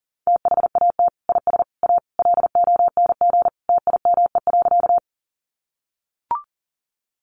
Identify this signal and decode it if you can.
Morse code, heard as a keyed tone, about 40 wpm, 700 Hz. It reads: THAT IS A LONG TIME.
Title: Sentences from Top 100 Words 40wpm